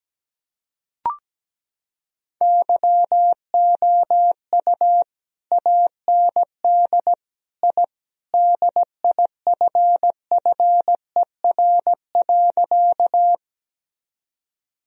Morse code, heard as a keyed tone, about 17 wpm, 700 Hz. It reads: YOU AND I DIFFER.
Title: Sentences from Top 1000 Words - Review Part 1 - 17wpm